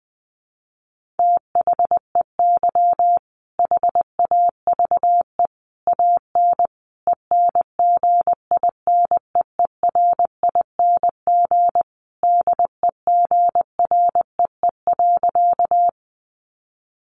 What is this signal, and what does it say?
Morse code, heard as a keyed tone, about 20 wpm, 700 Hz. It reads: THEY HAVE AN ENGINEERING DEGREE.
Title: Sentences from Top 1500 Words 20wpm